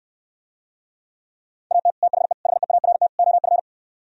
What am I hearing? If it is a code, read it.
Morse code, heard as a keyed tone, about 68 wpm, 700 Hz. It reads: AT THE HEART OF